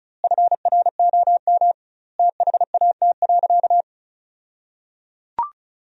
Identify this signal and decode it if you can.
Morse code, heard as a keyed tone, about 36 wpm, 700 Hz. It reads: FROM THAT.
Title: 2x Word-Spacing - Sentences from Top 300 Words - 35wpm